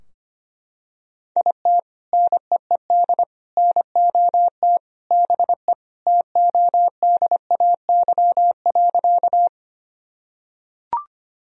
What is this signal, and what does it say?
Morse code, heard as a keyed tone, about 25 wpm, 700 Hz. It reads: IT NEED NOT BE TODAY.
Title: Sentences from Top 1000 Words - Review Part 4 - 25wpm